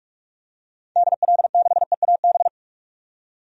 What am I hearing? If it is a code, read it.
Morse code, heard as a keyed tone, about 46 wpm, 700 Hz. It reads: DL6EAB
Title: DXCC-100 Call Signs - 6 Characters - 45wpm